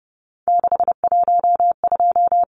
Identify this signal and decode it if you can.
Morse code, heard as a keyed tone, about 30 wpm, 700 Hz. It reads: 612
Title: Numbers - 3-Digit 30wpm